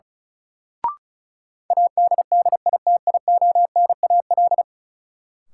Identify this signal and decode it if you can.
Morse code, heard as a keyed tone, about 34 wpm, 700 Hz. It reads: ADDITIONAL
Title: Sentences from Top 1000 Words - Review Part 4 - New Word Drill - 35wpm